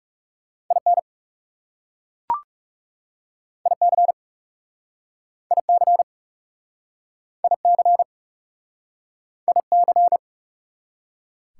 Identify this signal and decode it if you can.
Morse code, heard as a keyed tone, about 37 wpm, 690 Hz. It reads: IN IC IC IC IC
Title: Speed-Racing Prefixes and Suffixes 30wpm